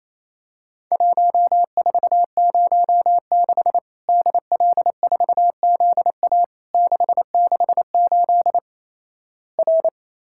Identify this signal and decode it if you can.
Morse code, heard as a keyed tone, about 28 wpm, 700 Hz. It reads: 1406 DL4ZA 668 R
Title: Worked All Europe DX Contest - QTC Exchange - 28wpm